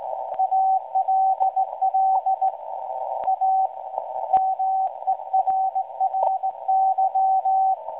Morse code, heard as a keyed tone, about 17 wpm, 760 Hz, with radio noise. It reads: ALL A MERRY
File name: WebSDR records / SAQ at PI4THT msg 0800-0804 17.2kHz